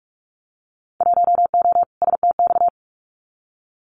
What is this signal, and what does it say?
Morse code, heard as a keyed tone, about 46 wpm, 700 Hz. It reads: 1O STX
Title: Winter Field Day Exchanges 45wpm